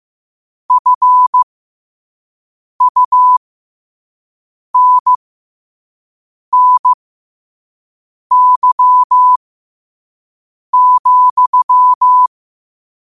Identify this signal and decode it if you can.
Morse code, heard as a keyed tone, about 15 wpm, spaced as for 5 wpm, 1000 Hz. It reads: FUNNY,
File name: qotdfiles / 5wpm